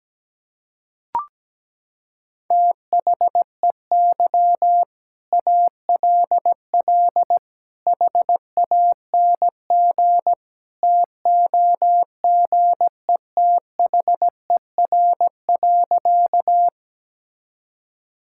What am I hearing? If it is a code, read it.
Morse code, heard as a keyed tone, about 17 wpm, 700 Hz. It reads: THEY ALL HANG TOGETHER.
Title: Sentences from Top 1600 Words 17wpm